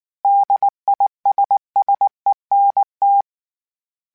Morse code, heard as a keyed tone, about 19 wpm, 800 Hz. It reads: DISSENT